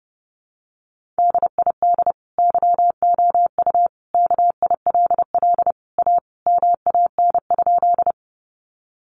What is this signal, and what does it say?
Morse code, heard as a keyed tone, about 28 wpm, 700 Hz. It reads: DID YOU KILL A MAN?
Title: Sentences from Top 1400 Words 30wpm